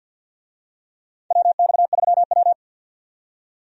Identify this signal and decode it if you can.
Morse code, heard as a keyed tone, about 50 wpm, 700 Hz. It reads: WX3W
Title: USA Call Signs - 4 Characters - 50wpm